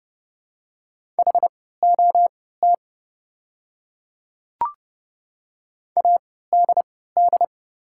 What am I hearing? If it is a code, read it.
Morse code, heard as a keyed tone, about 29 wpm, spaced as for 17 wpm, 700 Hz. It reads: HOT ADD
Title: 3x Character Spacing - Top 300 Words - 30wpm